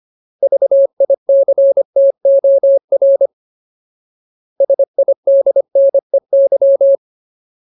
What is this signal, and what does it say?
Morse code, heard as a keyed tone, about 25 wpm, 550 Hz. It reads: VICTOR SIDNEY
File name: Ultimate Word List / 25WPM Ultimate